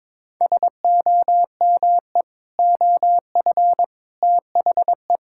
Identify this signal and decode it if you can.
Morse code, heard as a keyed tone, about 22 wpm, 700 Hz. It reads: SOME OF THE
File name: practice / 22wpm / Sets of 7 Words - 22wpm